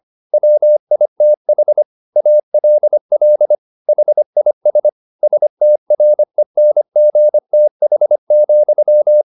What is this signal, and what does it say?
Morse code, heard as a keyed tone, about 25 wpm, 600 Hz. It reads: WITH ALL HIS STRENGTH,